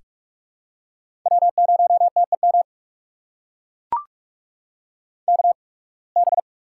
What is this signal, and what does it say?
Morse code, heard as a keyed tone, about 44 wpm, spaced as for 29 wpm, 700 Hz. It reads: W0TEM KB
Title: USA Call Signs - 5 Characters - 8x Character Spacing - 45wpm